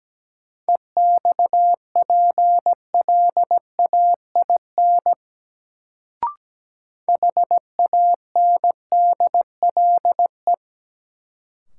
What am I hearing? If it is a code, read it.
Morse code, heard as a keyed tone, about 17 wpm, 700 Hz. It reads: EXPLAIN HANDLE
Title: Sentences from Top 1100 Words - New Word Drill - 17wpm